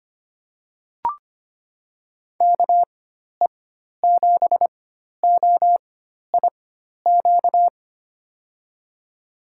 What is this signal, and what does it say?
Morse code, heard as a keyed tone, about 25 wpm, spaced as for 12 wpm, 700 Hz. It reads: KE7OIQ
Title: USA Call Signs - 6 Characters - 4x Character Spacing - 25wpm